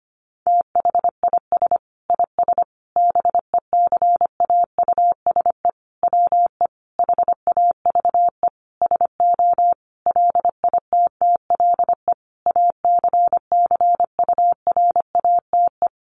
Morse code, heard as a keyed tone, about 25 wpm, 700 Hz. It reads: THIS IS BECAUSE WE HAVE SO LITTLE ACCURATE
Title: Sentences from Top 1200 Words 25wpm